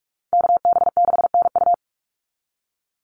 Morse code, heard as a keyed tone, about 44 wpm, 700 Hz. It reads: KB6NU
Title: Call Signs - 45wpm